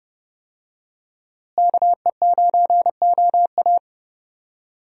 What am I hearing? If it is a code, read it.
Morse code, heard as a keyed tone, about 30 wpm, 700 Hz. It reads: KE9OA
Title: USA Call Signs - 5 Characters - Repeated 3x - 30wpm